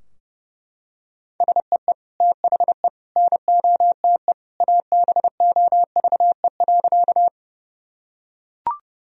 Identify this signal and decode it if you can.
Morse code, heard as a keyed tone, about 32 wpm, 700 Hz. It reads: SEE THE NOTE ABOVE.
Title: Sentences from Top 1000 Words - Review Part 2 - 30wpm